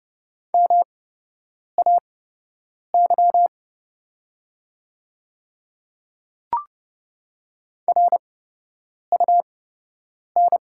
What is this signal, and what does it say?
Morse code, heard as a keyed tone, about 30 wpm, spaced as for 8 wpm, 700 Hz. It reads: MAY RUN
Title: 8x Character Spacing - Top 300 Words - 30wpm